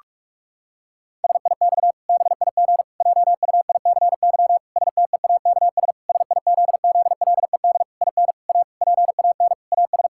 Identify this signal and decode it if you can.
Morse code, heard as a keyed tone, about 46 wpm, 700 Hz. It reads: SIX BIG JUICY STEAKS SIZZLED IN A PAN AS
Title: Pangram Challenge v2 - 45wpm